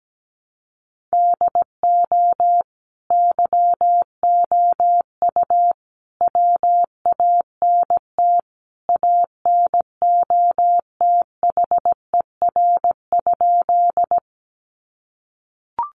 Morse code, heard as a keyed tone, about 17 wpm, 700 Hz. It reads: DO YOU WANT ANOTHER?
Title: Sentences from Top 600 Words 17wpm